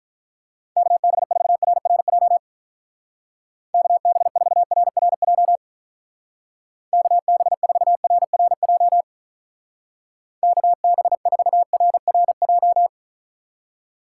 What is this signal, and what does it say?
Morse code, heard as a keyed tone, about 44 wpm, 700 Hz. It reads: KB4RRJ KB4RRJ KB4RRJ KB4RRJ